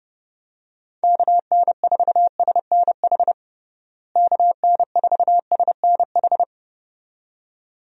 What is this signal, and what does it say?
Morse code, heard as a keyed tone, about 31 wpm, 700 Hz. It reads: KN4SNH KN4SNH
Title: USA Call Signs - 6 Characters - Repeated 2x - 30wpm